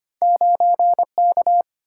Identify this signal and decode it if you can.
Morse code, heard as a keyed tone, about 33 wpm, 700 Hz. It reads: 9K